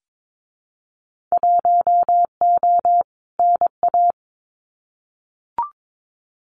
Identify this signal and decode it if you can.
Morse code, heard as a keyed tone, about 22 wpm, 700 Hz. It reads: 1O NA
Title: Winter Field Day Exchanges 22wpm